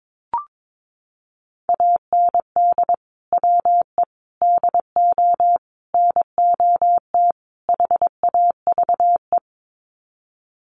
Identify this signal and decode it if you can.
Morse code, heard as a keyed tone, about 22 wpm, 700 Hz. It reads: AND WE DO NOT HAVE
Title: Sets of 5 Words Encore 22wpm